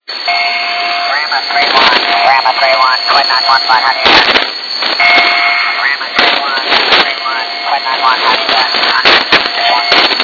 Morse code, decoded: none

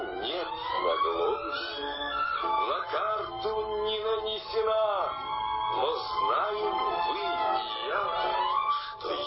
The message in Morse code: none

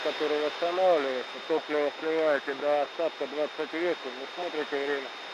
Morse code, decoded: none

